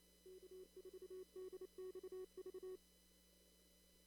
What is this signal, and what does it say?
Morse code, heard as a keyed tone, about 28 wpm, 380 Hz. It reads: K4DXV
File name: CW / kt5j wpx cw-op-k5pi